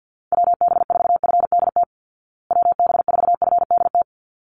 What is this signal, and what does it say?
Morse code, heard as a keyed tone, about 50 wpm, 700 Hz. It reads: WB4FDT WB4FDT
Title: Call Signs Repeated 3x - 50wpm